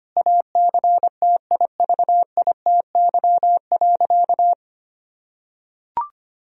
Morse code, heard as a keyed tone, about 25 wpm, 700 Hz. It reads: ACTIVITY.
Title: Sentences from Top 1000 Words - Review Part 4 - 25wpm